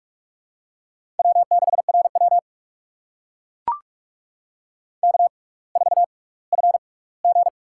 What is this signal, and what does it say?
Morse code, heard as a keyed tone, about 41 wpm, spaced as for 22 wpm, 700 Hz. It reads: W6RW K4FG
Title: USA Call Signs - 4 Characters - 6x Character Spacing - 45wpm